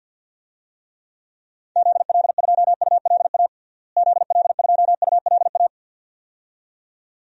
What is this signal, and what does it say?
Morse code, heard as a keyed tone, about 47 wpm, 700 Hz. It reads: ZL2ULA ZL2ULA
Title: DXCC-100 Call Signs - 6 Characters - Repeated 2x - 50wpm